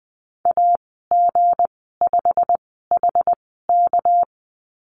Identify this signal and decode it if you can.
Morse code, heard as a keyed tone, about 20 wpm, spaced as for 14 wpm, 700 Hz. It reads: AG5HK
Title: 238 Lesson 38g - Letter q - Call Signs 20wpm